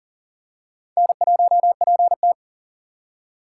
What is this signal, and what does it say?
Morse code, heard as a keyed tone, about 40 wpm, 700 Hz. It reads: N1PT
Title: USA Call Signs - 6x Character Spacing - 40wpm